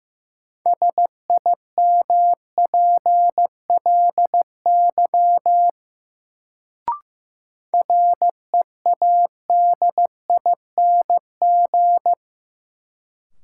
Morse code, heard as a keyed tone, about 15 wpm, 700 Hz. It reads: SIMPLY READING
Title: Sentences from Top 600 Words - New Word Drill - 15wpm